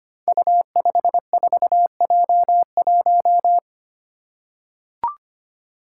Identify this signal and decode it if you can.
Morse code, heard as a keyed tone, about 25 wpm, 700 Hz. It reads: U54J1
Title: Five Challenging Letter-Number Combinations 25wpm